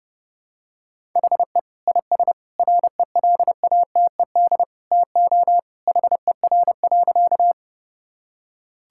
Morse code, heard as a keyed tone, about 32 wpm, 700 Hz. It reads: HE IS RELATED TO HER.
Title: Sentences from Top 1000 Words - Review Part 5 - 30wpm